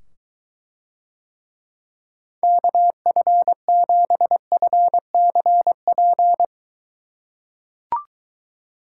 Morse code, heard as a keyed tone, about 23 wpm, 700 Hz. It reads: KF7FCP